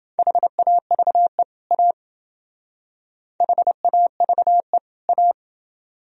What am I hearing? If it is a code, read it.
Morse code, heard as a keyed tone, about 28 wpm, 700 Hz. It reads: HAVE A HAVE A